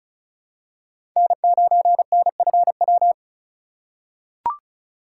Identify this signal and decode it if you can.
Morse code, heard as a keyed tone, about 36 wpm, 700 Hz. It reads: N9NFW
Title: USA Call Signs - 4x Character Spacing - 35wpm